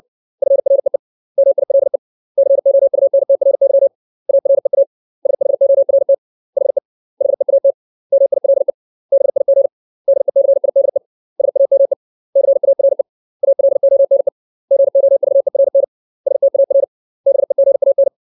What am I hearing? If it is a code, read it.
Morse code, heard as a keyed tone, about 60 wpm, 550 Hz. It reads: FREE MILE COUNTRY AREA SHORT HE HEAT MILE BIG DECIDE SAME CARE ALONE GOVERN STAR BEGAN
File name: 632 words / 632 Words 60WPM